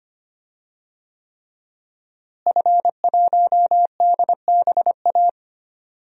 Morse code, heard as a keyed tone, about 25 wpm, 700 Hz. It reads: F1DBA
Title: DXCC-100 Call Signs - 5 Characters - Repeated 3x - 25wpm